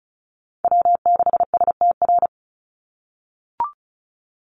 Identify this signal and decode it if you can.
Morse code, heard as a keyed tone, about 34 wpm, 700 Hz. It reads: W6STR
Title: NA SOTA Chaser Callsigns 35wpm